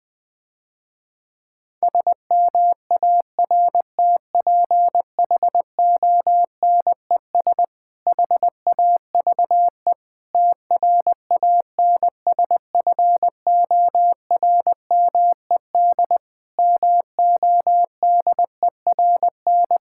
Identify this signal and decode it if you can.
Morse code, heard as a keyed tone, about 20 wpm, 700 Hz. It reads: SMARTPHONES HAVE TRANSFORMED MODERN
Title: Difficult Sentences with Hints - 20wpm